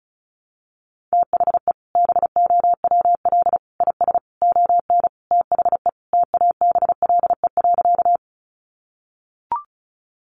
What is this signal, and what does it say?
Morse code, heard as a keyed tone, about 34 wpm, 700 Hz. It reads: THE BOWL IS ON THE TABLE.
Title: Sentences from Top 1200 Words 35wpm